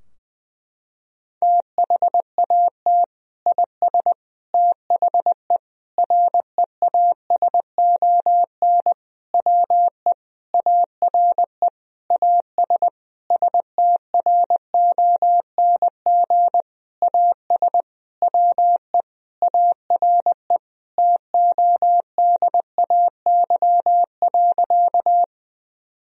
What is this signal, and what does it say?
Morse code, heard as a keyed tone, about 21 wpm, 700 Hz. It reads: THAT IS THE REASON WE ARE AS STRONG AS WE ARE TODAY.
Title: Sentences from Top 1000 Words - Review Part 4 - 20wpm